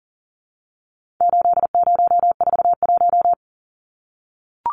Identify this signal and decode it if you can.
Morse code, heard as a keyed tone, about 40 wpm, 700 Hz. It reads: 8041
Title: Numbers - 4-Digit 40wpm